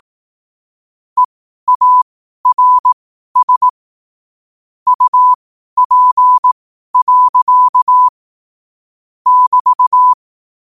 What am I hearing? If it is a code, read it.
Morse code, heard as a keyed tone, about 18 wpm, spaced as for 12 wpm, 1000 Hz. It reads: EARS UP. <BT>
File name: qotdfiles / 12wpm